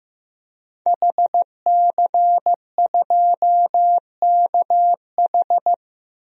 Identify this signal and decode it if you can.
Morse code, heard as a keyed tone, about 15 wpm, 700 Hz. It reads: HC2KH
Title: DXCC-100 Call Signs - 5 Characters - Repeated 2x - 15wpm